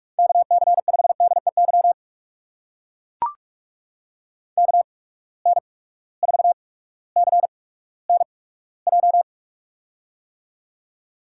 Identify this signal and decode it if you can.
Morse code, heard as a keyed tone, about 44 wpm, spaced as for 19 wpm, 700 Hz. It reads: KC5DEY KN4CNJ
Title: USA Call Signs - 8x Character Spacing - 45wpm